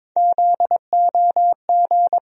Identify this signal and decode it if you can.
Morse code, heard as a keyed tone, about 22 wpm, 700 Hz. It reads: ZOG
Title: USA Call Sign Postfixes - 3 Letters 22wpm